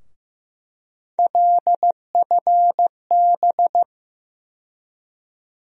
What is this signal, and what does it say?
Morse code, heard as a keyed tone, about 15 wpm, 700 Hz. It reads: LFB
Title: USA Call Sign Postfixes - 3 Letters 15wpm